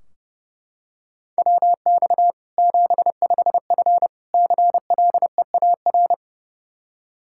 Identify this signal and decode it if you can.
Morse code, heard as a keyed tone, about 30 wpm, 700 Hz. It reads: WX 75F CLEAR
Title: Practice QSO - Exchange 2 - Info - 30wpm